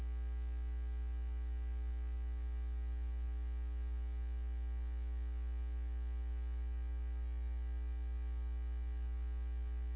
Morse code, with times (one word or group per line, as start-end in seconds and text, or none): none